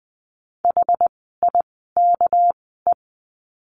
Morse code, heard as a keyed tone, about 20 wpm, spaced as for 15 wpm, 700 Hz. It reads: HIKE
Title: 199 Lesson 32f - Letter k - Words 20wpm